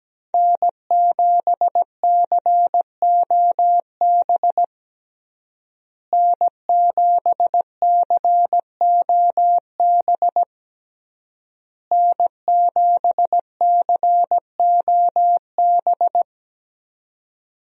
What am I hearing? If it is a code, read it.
Morse code, heard as a keyed tone, about 17 wpm, 700 Hz. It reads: N7COB N7COB N7COB
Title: USA Call Signs - 5 Characters - Repeated 3x - 17wpm